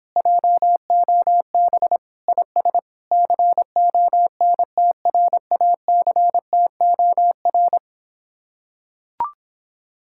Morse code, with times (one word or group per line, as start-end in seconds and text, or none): 0.2-2.0 JOB
2.3-2.8 IS
3.1-7.8 CONTRACTOR